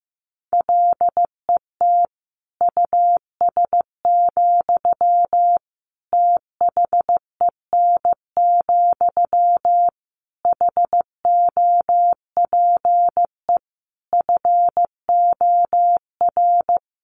Morse code, heard as a keyed tone, about 15 wpm, 700 Hz. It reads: LET US, THEN, HOPE FOR
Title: Sentences from Top 1000 Words 15wpm